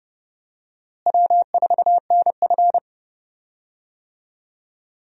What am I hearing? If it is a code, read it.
Morse code, heard as a keyed tone, about 31 wpm, 700 Hz. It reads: W4NF
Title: USA Call Signs - 4 Characters - Repeated 2x - 30wpm